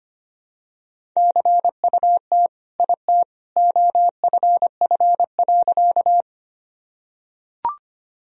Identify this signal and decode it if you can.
Morse code, heard as a keyed tone, about 25 wpm, 700 Hz. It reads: CUT IT OFF.
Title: Sentences from Top 1000 Words - Review Part 1 - 25wpm